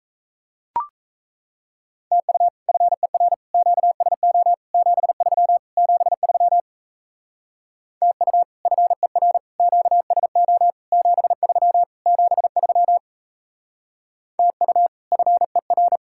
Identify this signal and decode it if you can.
Morse code, heard as a keyed tone, about 36 wpm, 700 Hz. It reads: TU FER QSO 73 73 TU FER QSO 73 73 TU FER